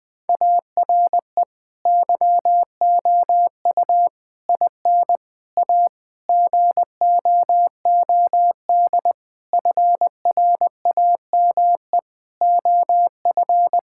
Morse code, heard as a keyed tone, about 21 wpm, 700 Hz. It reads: ARE YOU IN A GOOD FRAME OF
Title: Sentences from Top 1000 Words - Review Part 6 - 20wpm